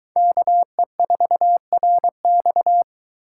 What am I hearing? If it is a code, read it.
Morse code, heard as a keyed tone, about 23 wpm, 700 Hz. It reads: KE4RX